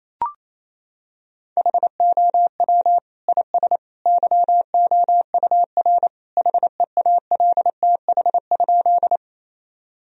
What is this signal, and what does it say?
Morse code, heard as a keyed tone, about 27 wpm, 700 Hz. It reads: HOW IS YOUR HEALTH?
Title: Sentences from Top 1000 Words - Review Part 3 - 28wpm